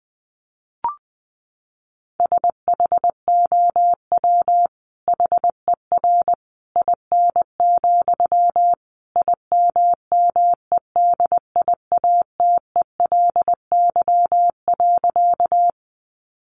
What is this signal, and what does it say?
Morse code, heard as a keyed tone, about 20 wpm, 700 Hz. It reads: SHOW HER IN, IMMEDIATELY.
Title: Sentences from Top 1100 Words 20wpm